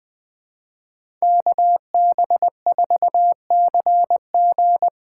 Morse code, heard as a keyed tone, about 20 wpm, 700 Hz. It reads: KB4CG
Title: K1USN SST QSO Practice - 20wpm